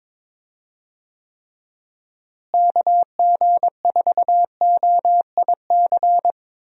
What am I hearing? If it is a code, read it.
Morse code, heard as a keyed tone, about 21 wpm, 700 Hz. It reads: KG4OIC
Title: USA Call Signs - 6 Characters - Repeated 2x - 22wpm